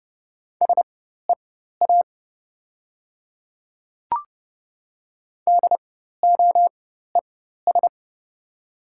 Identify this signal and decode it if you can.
Morse code, heard as a keyed tone, about 29 wpm, spaced as for 14 wpm, 690 Hz. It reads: SEA DOES
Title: 4x Character Spacing - Top 300 Words - 30wpm